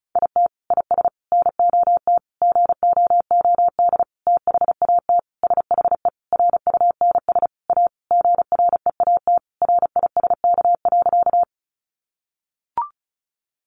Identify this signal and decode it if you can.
Morse code, heard as a keyed tone, about 34 wpm, 700 Hz. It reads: IT IS NOT GOOD THAT SHE RUNS A GREAT RISK.
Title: Sentences from Top 600 Words 35wpm